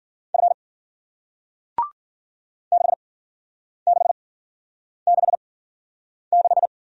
Morse code, heard as a keyed tone, about 54 wpm, spaced as for 9 wpm, 700 Hz. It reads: F 6666